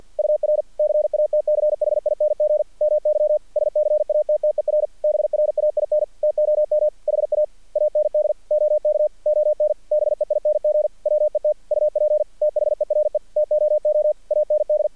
Morse code, heard as a keyed tone, about 46 wpm, 610 Hz. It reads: UR CATCHING MY SCATTER BRAIN TOM HA AND OK ON BEING WET UP THERE TOO AND
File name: archived / CRSnet